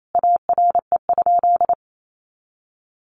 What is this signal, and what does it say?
Morse code, heard as a keyed tone, about 28 wpm, 700 Hz. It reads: ARE?
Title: A Sound Of Thunder 28wpm